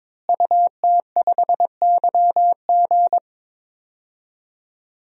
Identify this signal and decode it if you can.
Morse code, heard as a keyed tone, about 22 wpm, 700 Hz. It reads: UT5YG
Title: DXCC-100 Call Signs - 5 Characters - Repeated 3x - 22wpm